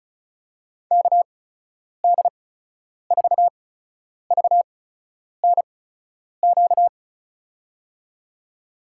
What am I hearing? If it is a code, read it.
Morse code, heard as a keyed tone, about 36 wpm, spaced as for 10 wpm, 700 Hz. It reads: KD4VNQ
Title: USA Call Signs - 6 Characters - 8x Character Spacing - 35wpm